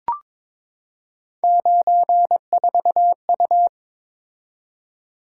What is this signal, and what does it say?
Morse code, heard as a keyed tone, about 22 wpm, 700 Hz. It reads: 94U
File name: practice / 22wpm / Three Letter-Number Combinations - 22wpm